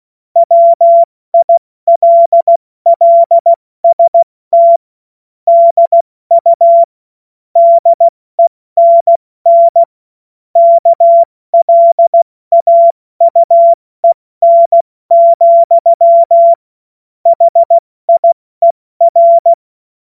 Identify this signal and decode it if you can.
Morse code, heard as a keyed tone, about 16 wpm, 680 Hz. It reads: WILLST DU DENN KLAUEN, HIER
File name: Morsekurs / Witze / Witz 28.November 2023 70 F80